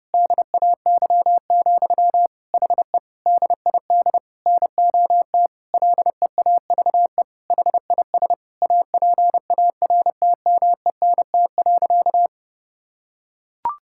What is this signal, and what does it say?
Morse code, heard as a keyed tone, about 30 wpm, 700 Hz. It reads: DAY, HE DID NOT LEAVE HIS APARTMENT.
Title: Sentences from Top 1000 Words - Review Part 6 - 30wpm